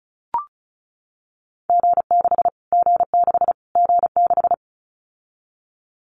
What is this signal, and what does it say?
Morse code, heard as a keyed tone, about 34 wpm, 700 Hz. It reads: G6 G6 G6
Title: Callsign Prefixes for Top 25 DXCC - Repeated 3x - 35wpm